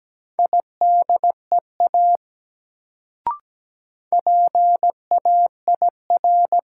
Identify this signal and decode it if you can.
Morse code, heard as a keyed tone, about 17 wpm, 700 Hz. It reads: IDEA PAIR
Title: Sets of 3 Words - New Word Drill - 17wpm